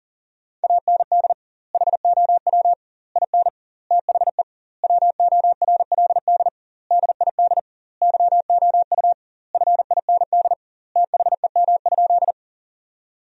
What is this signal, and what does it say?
Morse code, heard as a keyed tone, about 40 wpm, 700 Hz. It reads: AND HOW IN THE WORLD DID YOU FIND THEM?
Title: 2x Word-Spacing - Sentences from Top 200 Words - 40wpm